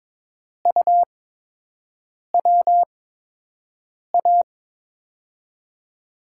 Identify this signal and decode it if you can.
Morse code, heard as a keyed tone, about 21 wpm, spaced as for 6 wpm, 700 Hz. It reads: UWA
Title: USA Call Signs - 6 Characters - 8x Character Spacing - 22wpm